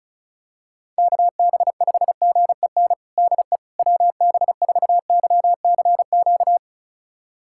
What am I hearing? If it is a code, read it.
Morse code, heard as a keyed tone, about 34 wpm, 700 Hz. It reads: KB5GEN DE WB4YCQ
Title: USA Callsign DE Callsign - 6 Characters - 35wpm